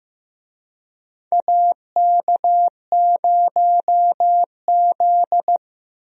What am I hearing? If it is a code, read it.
Morse code, heard as a keyed tone, about 15 wpm, 700 Hz. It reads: AK0Z
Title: USA Call Signs - 4 Characters - Repeated 2x - 15wpm